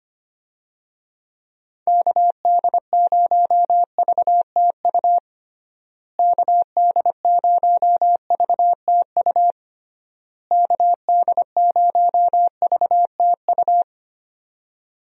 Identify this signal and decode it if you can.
Morse code, heard as a keyed tone, about 25 wpm, 700 Hz. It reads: KD0VTU KD0VTU KD0VTU
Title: USA Call Signs - 6 Characters - Repeated 3x - 25wpm